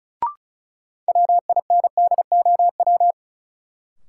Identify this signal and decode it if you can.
Morse code, heard as a keyed tone, about 34 wpm, 700 Hz. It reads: WINDOW
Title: Sentences from Top 1600 Words - New Word Drill - 35wpm